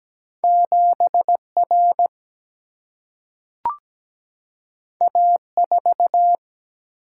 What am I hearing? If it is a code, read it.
Morse code, heard as a keyed tone, about 17 wpm, 700 Hz. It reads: 7R A4
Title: Single-Letter Single-Number 17wpm